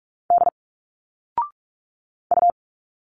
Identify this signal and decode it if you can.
Morse code, heard as a keyed tone, about 46 wpm, 700 Hz. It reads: D U
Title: Single Letters 45wpm